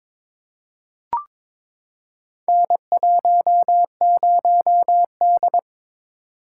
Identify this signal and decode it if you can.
Morse code, heard as a keyed tone, about 23 wpm, 700 Hz. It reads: N10D
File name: practice / 22wpm / Four Letter-Number Combinations - 22wpm